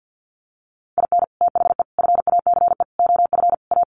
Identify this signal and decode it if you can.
Morse code, heard as a keyed tone, about 50 wpm, 700 Hz. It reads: IN THE FACE OF A
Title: Sets of 5 Words Encore 50wpm